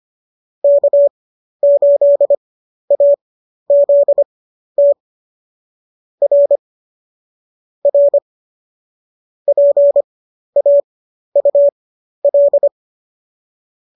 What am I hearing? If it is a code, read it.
Morse code, heard as a keyed tone, about 25 wpm, spaced as for 12 wpm, 590 Hz. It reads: K8AZT R R PAUL